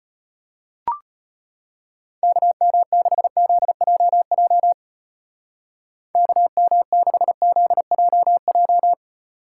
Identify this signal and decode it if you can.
Morse code, heard as a keyed tone, about 31 wpm, 700 Hz. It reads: KM6ZJJ KM6ZJJ
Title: Speed Racing - USA 2x3 Call Signs 25wpm